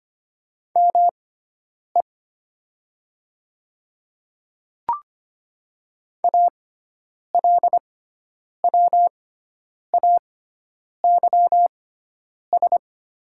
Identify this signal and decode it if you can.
Morse code, heard as a keyed tone, about 25 wpm, spaced as for 9 wpm, 700 Hz. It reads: ME ALWAYS
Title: 6x Character Spacing - Top 300 Words - 25wpm